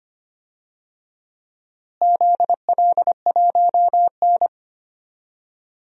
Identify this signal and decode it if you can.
Morse code, heard as a keyed tone, about 25 wpm, 700 Hz. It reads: ZL1N
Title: DXCC-100 Call Signs - 4 Characters - Repeated 2x - 25wpm